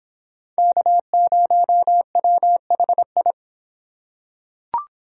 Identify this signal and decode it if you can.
Morse code, heard as a keyed tone, about 26 wpm, 700 Hz. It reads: K0WHI